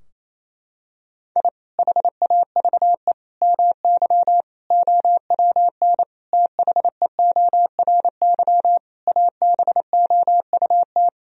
Morse code, heard as a keyed tone, about 28 wpm, 700 Hz. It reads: I HAVE MY OWN THEORY ABOUT
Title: Sentences from Top 1000 Words - Review Part 3 - 28wpm